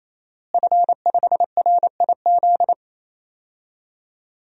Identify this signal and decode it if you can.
Morse code, heard as a keyed tone, about 28 wpm, 700 Hz. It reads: F5RIZ
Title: DXCC-100 Call Signs - 5 Characters - Repeated 3x - 28wpm